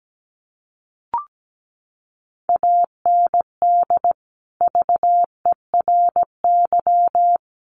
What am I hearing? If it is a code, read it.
Morse code, heard as a keyed tone, about 17 wpm, 700 Hz. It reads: AND VERY
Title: Sentences from Top 1300 Words 17wpm